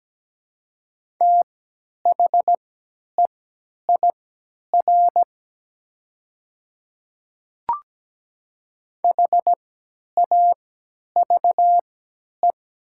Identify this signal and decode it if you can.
Morse code, heard as a keyed tone, about 17 wpm, spaced as for 10 wpm, 700 Hz. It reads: THEIR HAVE
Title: 3x Character Spacing - Top 100 Words - 17wpm